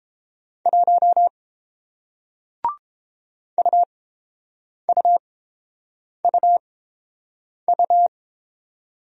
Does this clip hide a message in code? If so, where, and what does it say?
Yes: Morse code from 0.7 to 8.1 s, heard as a keyed tone, about 30 wpm, spaced as for 7 wpm, 700 Hz. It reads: 1 UUUU